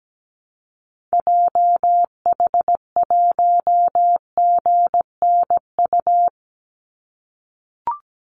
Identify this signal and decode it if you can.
Morse code, heard as a keyed tone, about 17 wpm, 700 Hz. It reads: JH1GNU
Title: Call Signs - 17wpm